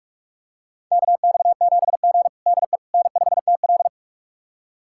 Morse code, heard as a keyed tone, about 46 wpm, 700 Hz. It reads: KX7G DE N5TL
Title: USA Callsign DE Callsign - 4 Characters - 45wpm